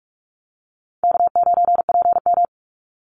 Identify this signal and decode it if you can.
Morse code, heard as a keyed tone, about 46 wpm, 700 Hz. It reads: K9PM
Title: NA SOTA Activator Callsigns 45wpm